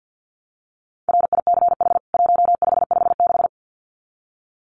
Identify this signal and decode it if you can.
Morse code, heard as a keyed tone, about 50 wpm, 700 Hz. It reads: RICH 1556
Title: CWT Exchange 50wpm